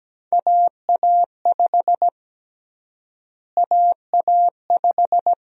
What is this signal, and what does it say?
Morse code, heard as a keyed tone, about 17 wpm, 700 Hz. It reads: AA5 AA5
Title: USA Call Sign Prefixes - Repeated 2x - 17wpm